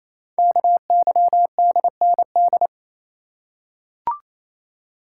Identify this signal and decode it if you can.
Morse code, heard as a keyed tone, about 28 wpm, 700 Hz. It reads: KYDND
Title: Code Groups - 5 Letters - 28wpm